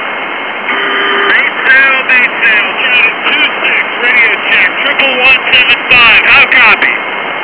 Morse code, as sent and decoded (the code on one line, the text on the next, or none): none